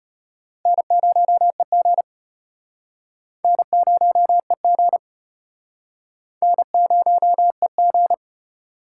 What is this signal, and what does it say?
Morse code, heard as a keyed tone, about 33 wpm, 700 Hz. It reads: N0EG N0EG N0EG